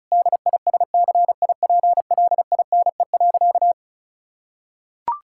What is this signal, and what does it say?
Morse code, heard as a keyed tone, about 34 wpm, 700 Hz. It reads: DISCIPLINE.
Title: Sentences from Top 1000 Words - Review Part 6 - 35wpm